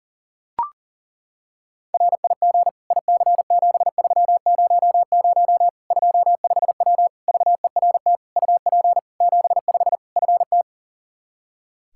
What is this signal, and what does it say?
Morse code, heard as a keyed tone, about 40 wpm, 700 Hz. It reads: RIG IC7300 25W VERT UP 75 FT
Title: Practice QSO - Exchange 2 - Info - 40wpm